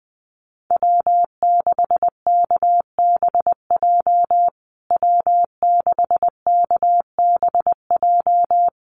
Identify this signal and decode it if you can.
Morse code, heard as a keyed tone, about 20 wpm, 700 Hz. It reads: W6KBJ W6KBJ
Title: USA Call Signs Repeated 3x 20wpm